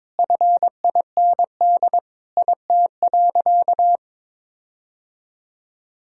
Morse code, heard as a keyed tone, about 22 wpm, 700 Hz. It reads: FIND IT.
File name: practice / 22wpm / Sentences with 4-Word Phrases - 22wpm